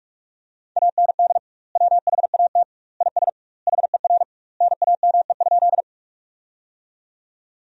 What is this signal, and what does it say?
Morse code, heard as a keyed tone, about 42 wpm, 700 Hz. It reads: AND WHAT IS HER NAME?
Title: 2x Word-Spacing - Sentences from Top 200 Words - 45wpm